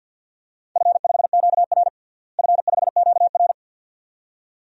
Morse code, heard as a keyed tone, about 53 wpm, 700 Hz. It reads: F5QR F5QR
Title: DXCC-100 Call Signs - 4 Characters - Repeated 2x - 50wpm